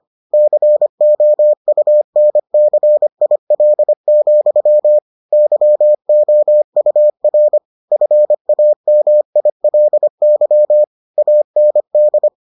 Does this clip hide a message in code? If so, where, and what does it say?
Yes: Morse code from 0.3 to 12.3 s, heard as a keyed tone, about 25 wpm, 600 Hz. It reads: COUNCIL, YOUR FAMILY AND